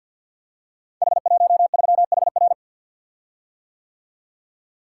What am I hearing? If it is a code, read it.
Morse code, heard as a keyed tone, about 45 wpm, 700 Hz. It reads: H13HR
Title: Code Groups - 5 Letter:Numbers - 50wpm